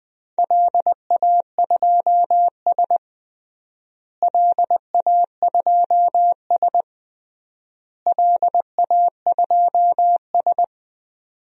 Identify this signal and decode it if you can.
Morse code, heard as a keyed tone, about 20 wpm, 700 Hz. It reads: LA2S LA2S LA2S
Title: DXCC-100 Call Signs - 4 Characters - Repeated 3x - 20wpm